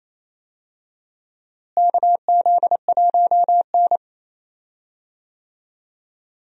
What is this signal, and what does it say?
Morse code, heard as a keyed tone, about 28 wpm, 700 Hz. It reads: KZ1N
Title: USA Call Signs - 4 Characters - Repeated 2x - 28wpm